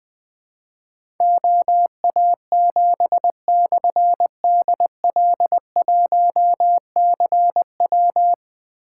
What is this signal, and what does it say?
Morse code, heard as a keyed tone, about 20 wpm, 700 Hz. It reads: OA7/DL1CW
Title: Difficult Call Signs 20wpm